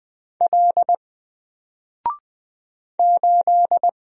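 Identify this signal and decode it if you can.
Morse code, heard as a keyed tone, about 20 wpm, 700 Hz. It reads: L 8
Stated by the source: Single letter and number Morse drill at 20 wpm, from N4LCD.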